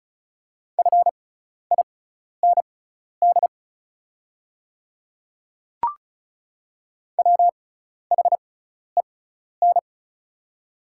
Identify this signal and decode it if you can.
Morse code, heard as a keyed tone, about 36 wpm, spaced as for 12 wpm, 700 Hz. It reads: FIND WHEN
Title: 6x Character Spacing - Top 300 Words - 35wpm